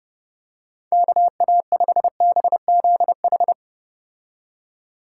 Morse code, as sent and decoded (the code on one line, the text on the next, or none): -.- .- ..... -... --.. ....
KA5BZH